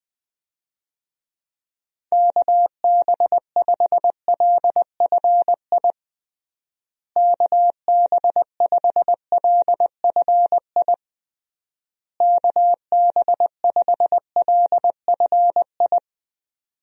Morse code, heard as a keyed tone, about 20 wpm, 700 Hz. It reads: KB5LFI KB5LFI KB5LFI
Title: USA Call Signs - 6 Characters - Repeated 3x - 20wpm